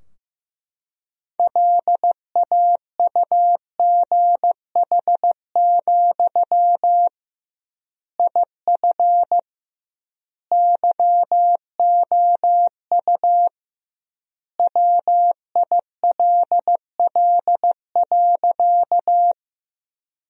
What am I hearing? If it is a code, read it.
Morse code, heard as a keyed tone, about 15 wpm, 700 Hz. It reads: LAUGH, IF YOU WILL.
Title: 2x Word-Spacing - Sentences from Top 500 Words - 15wpm